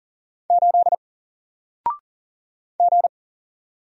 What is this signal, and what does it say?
Morse code, heard as a keyed tone, about 40 wpm, 700 Hz. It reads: 8 G
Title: Single Character (Letter:Number:Punctuation:Prosign) Alt 40wpm